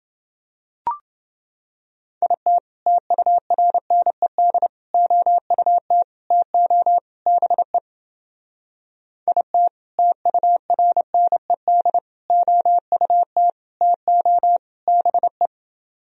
Morse code, heard as a keyed tone, about 30 wpm, 700 Hz. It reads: IT TURNED OUT TO BE IT TURNED OUT TO BE